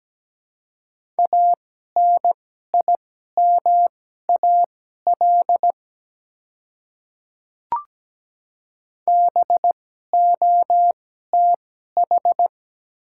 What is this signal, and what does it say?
Morse code, heard as a keyed tone, about 17 wpm, spaced as for 12 wpm, 690 Hz. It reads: ANIMAL BOTH
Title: 2x Character Spacing - Top 300 Words - 17wpm